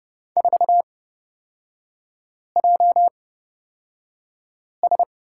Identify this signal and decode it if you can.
Morse code, heard as a keyed tone, about 29 wpm, spaced as for 5 wpm, 700 Hz. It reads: 4JS
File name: Lesson 37e - Letter x - Three Characters 30wpm